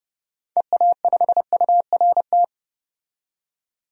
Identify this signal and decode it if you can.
Morse code, heard as a keyed tone, about 31 wpm, 700 Hz. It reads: EA5URT
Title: DXCC-100 Call Signs - 6 Characters - 30wpm